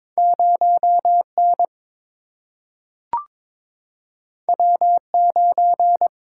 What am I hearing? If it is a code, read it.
Morse code, heard as a keyed tone, about 22 wpm, 700 Hz. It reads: 0 N W 9
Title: Single-Letter Single-Number 22wpm